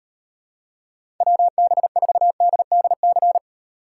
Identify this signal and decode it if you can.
Morse code, heard as a keyed tone, about 38 wpm, 700 Hz. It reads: WB4DDC